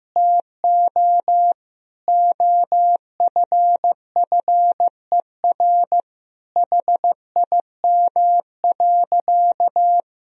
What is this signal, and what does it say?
Morse code, heard as a keyed tone, about 16 wpm, 700 Hz. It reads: TO OFFER HIM.
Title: Sentences from Top 1000 Words - Review Part 4 - 15wpm